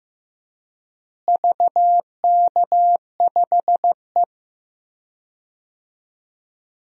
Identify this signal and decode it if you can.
Morse code, heard as a keyed tone, about 15 wpm, 700 Hz. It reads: VK5E
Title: DXCC-100 Call Signs - 4 Characters - Repeated 2x - 15wpm